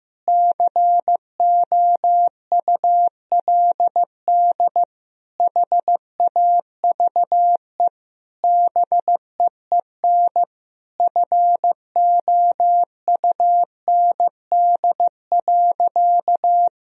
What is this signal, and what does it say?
Morse code, heard as a keyed tone, about 15 wpm, 700 Hz. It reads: COULD HAVE BEEN FOUND.
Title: Sentences from Top 1000 Words - Review Part 4 - 15wpm